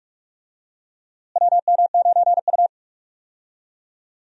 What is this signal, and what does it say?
Morse code, heard as a keyed tone, about 46 wpm, 700 Hz. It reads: WM9U
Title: USA Call Signs - 4 Characters - Repeated 2x - 45wpm